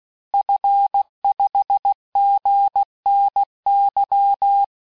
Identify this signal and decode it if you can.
Morse code, heard as a keyed tone, about 16 wpm, 800 Hz. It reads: F5GNY
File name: cours-du-lundi files / cours-1580-fin-QSO